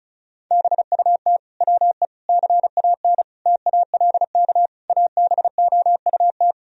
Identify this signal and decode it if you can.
Morse code, heard as a keyed tone, about 34 wpm, 700 Hz. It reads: BUT WE CAN TALK ABOUT
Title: Sentences from Top 1000 Words - Review Part 4 - 35wpm